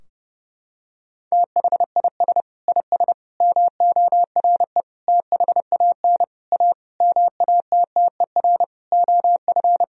Morse code, heard as a keyed tone, about 30 wpm, 700 Hz. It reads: THIS IS MORE THAN A MATTER OF
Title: Sentences from Top 1000 Words - Review Part 3 - 30wpm